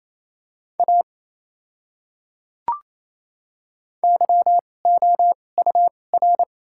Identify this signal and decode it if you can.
Morse code, heard as a keyed tone, about 28 wpm, spaced as for 20 wpm, 700 Hz. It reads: A YOUR
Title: 2x Character Spacing - Top 100 Words - 28wpm